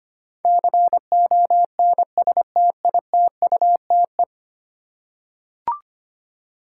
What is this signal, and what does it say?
Morse code, heard as a keyed tone, about 25 wpm, 700 Hz. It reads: CONSTITUTE
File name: 10 Letter Words - First 3 Letters Given - 25wpm